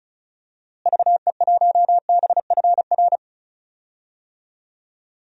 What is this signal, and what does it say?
Morse code, heard as a keyed tone, about 34 wpm, 700 Hz. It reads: VE1BFR
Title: DXCC-100 Call Signs - 6 Characters - Repeated 2x - 35wpm